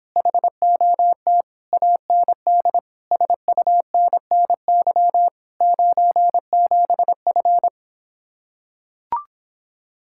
Morse code, heard as a keyed tone, about 26 wpm, 700 Hz. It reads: HOT AND SUNNY 97F